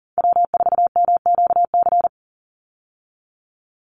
Morse code, heard as a keyed tone, about 40 wpm, 700 Hz. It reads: W4MQC
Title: Call Signs - 40wpm